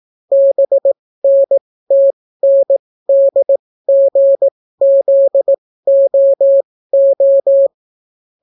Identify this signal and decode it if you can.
Morse code, heard as a keyed tone, about 18 wpm, spaced as for 15 wpm, 550 Hz. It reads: BNTNDGZOO